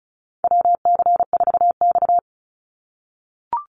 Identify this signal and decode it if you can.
Morse code, heard as a keyed tone, about 34 wpm, 700 Hz. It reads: WC4X
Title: Call Signs - 35wpm